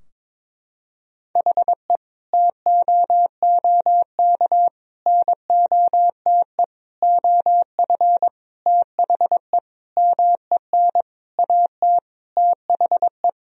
Morse code, heard as a keyed tone, about 22 wpm, 700 Hz. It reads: HE TOOK NOTE OF THE MEN AT THE
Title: Sentences from Top 1000 Words - Review Part 2 - 22wpm